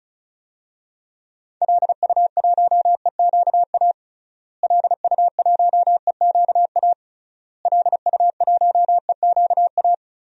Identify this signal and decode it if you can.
Morse code, heard as a keyed tone, about 36 wpm, 700 Hz. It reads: LU1EQA LU1EQA LU1EQA
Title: DXCC-100 Call Signs - 6 Characters - Repeated 3x - 35wpm